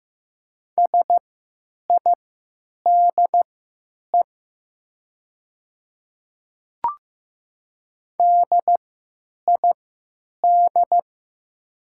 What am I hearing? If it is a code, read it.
Morse code, heard as a keyed tone, about 15 wpm, spaced as for 9 wpm, 700 Hz. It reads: SIDE DID
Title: 3x Character Spacing - Top 100 Words - 15wpm